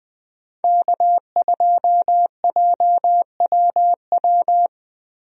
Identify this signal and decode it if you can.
Morse code, heard as a keyed tone, about 20 wpm, 700 Hz. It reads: K2JWW
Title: USA Call Signs - 5 Characters - 20wpm